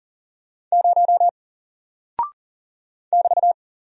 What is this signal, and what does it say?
Morse code, heard as a keyed tone, about 40 wpm, 700 Hz. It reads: 0 <BT>
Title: Single Character (Letter:Number:Punctuation:Prosign) Alt 40wpm